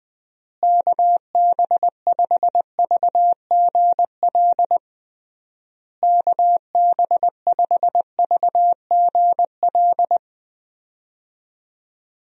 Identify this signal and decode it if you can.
Morse code, heard as a keyed tone, about 20 wpm, 700 Hz. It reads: KB5VGL KB5VGL
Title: USA Call Signs - 6 Characters - Repeated 3x - 20wpm